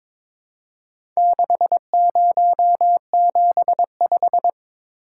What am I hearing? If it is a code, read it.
Morse code, heard as a keyed tone, about 22 wpm, 700 Hz. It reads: 6075
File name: practice / 22wpm / Four Letter-Number Combinations - 22wpm